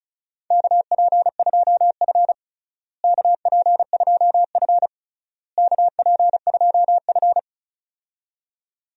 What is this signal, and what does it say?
Morse code, heard as a keyed tone, about 36 wpm, 690 Hz. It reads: KP2F KP2F KP2F
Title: DXCC-100 Call Signs - 4 Characters - Repeated 3x - 35wpm